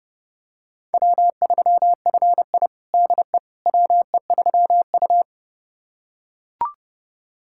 Morse code, heard as a keyed tone, about 32 wpm, 700 Hz. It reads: W3FI DE WE3U
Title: USA Callsign DE Callsign - 4 Characters - 30wpm